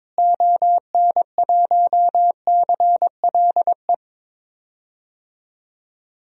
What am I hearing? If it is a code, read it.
Morse code, heard as a keyed tone, about 22 wpm, 700 Hz. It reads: ON1CLE
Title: DXCC-100 Call Signs - 6 Characters - Repeated 2x - 22wpm